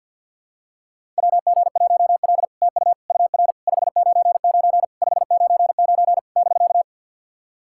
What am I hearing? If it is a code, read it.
Morse code, heard as a keyed tone, about 50 wpm, 700 Hz. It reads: WG1L TU UR 599 599 <BK>
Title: SOTA QSO Practice 50wpm